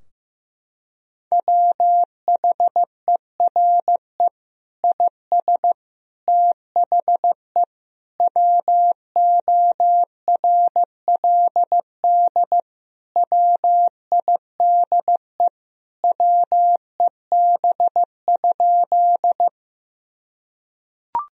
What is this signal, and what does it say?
Morse code, heard as a keyed tone, about 15 wpm, 700 Hz. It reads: WHERE IS THE WORLD WIDE WEB?
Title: Sentences from Top 1000 Words - Review Part 4 - 15wpm